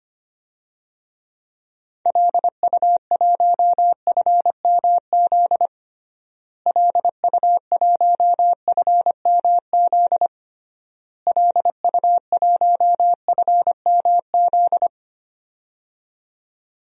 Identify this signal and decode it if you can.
Morse code, heard as a keyed tone, about 25 wpm, 700 Hz. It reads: LU1FMZ LU1FMZ LU1FMZ
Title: DXCC-100 Call Signs - 6 Characters - Repeated 3x - 25wpm